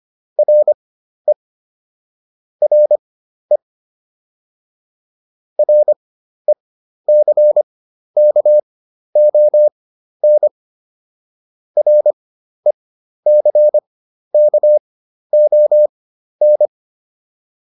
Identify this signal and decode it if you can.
Morse code, heard as a keyed tone, about 25 wpm, spaced as for 12 wpm, 600 Hz. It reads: RE RE RECKON RECKON